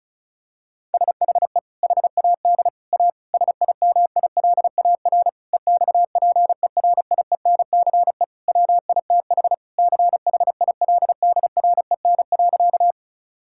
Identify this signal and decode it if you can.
Morse code, heard as a keyed tone, about 36 wpm, 700 Hz. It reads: SHE HAD A SIMILAR EXPERIENCE WITH CHILDREN.
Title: Sentences from Top 1000 Words - Review Part 5 - 35wpm